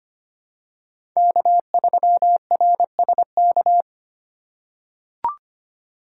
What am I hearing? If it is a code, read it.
Morse code, heard as a keyed tone, about 25 wpm, 700 Hz. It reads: K3RSK
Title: USA Call Signs - 5 Characters - 2x Character Spacing - 25wpm